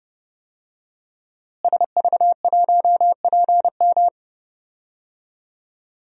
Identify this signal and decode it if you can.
Morse code, heard as a keyed tone, about 32 wpm, 700 Hz. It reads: SV1PM
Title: DXCC-100 Call Signs - 5 Characters - Repeated 3x - 30wpm